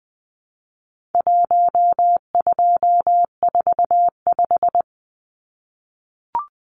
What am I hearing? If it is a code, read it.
Morse code, heard as a keyed tone, about 20 wpm, 700 Hz. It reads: 1245
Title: Numbers - Contest Style - 4-Digit 20wpm